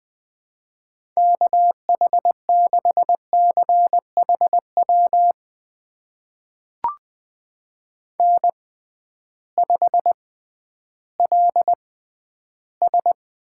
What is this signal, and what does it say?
Morse code, heard as a keyed tone, about 20 wpm, spaced as for 12 wpm, 700 Hz. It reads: KH6CHW N5LS
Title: USA Call Signs - 6x Character Spacing - 20wpm